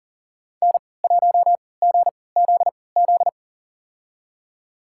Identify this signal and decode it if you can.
Morse code, heard as a keyed tone, about 40 wpm, spaced as for 23 wpm, 700 Hz. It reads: N1GZZ
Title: USA Call Signs - 5 Characters - 3x Character Spacing - 40wpm